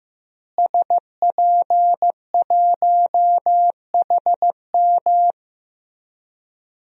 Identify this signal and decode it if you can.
Morse code, heard as a keyed tone, about 15 wpm, 700 Hz. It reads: SP1HM
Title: DXCC-100 Call Signs - 5 Characters - Repeated 2x - 15wpm